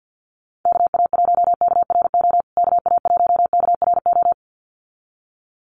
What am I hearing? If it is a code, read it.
Morse code, heard as a keyed tone, about 50 wpm, 700 Hz. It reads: KA1KRO KA1KRO
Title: USA Call Signs Repeated 2x 50wpm